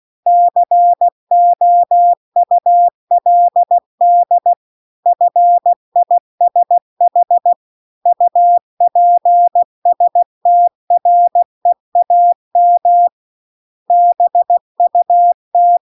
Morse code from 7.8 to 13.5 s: UPSTREAM